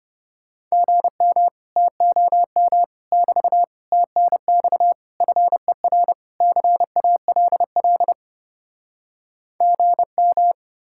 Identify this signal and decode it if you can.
Morse code, heard as a keyed tone, about 33 wpm, 700 Hz. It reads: GM TOM <BT> TNX FER CALL GM